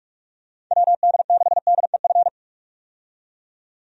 Morse code, heard as a keyed tone, about 41 wpm, 700 Hz. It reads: WD6DEF
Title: USA Call Signs - 6 Characters - 45wpm